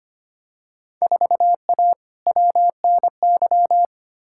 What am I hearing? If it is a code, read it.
Morse code, heard as a keyed tone, about 25 wpm, 700 Hz. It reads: 4A WNY
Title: Field Day Exchanges Encore - 25wpm